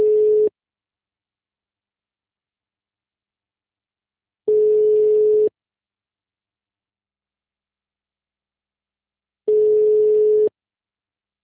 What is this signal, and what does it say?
Ringback tone (Europe, 425 Hz).
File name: out- -2012- - - .6